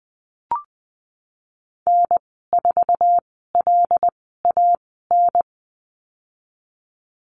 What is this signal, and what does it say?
Morse code, heard as a keyed tone, about 20 wpm, spaced as for 15 wpm, 700 Hz. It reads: N4LAN
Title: 128 Lesson 21g - Number 6 - Call Signs 20wpm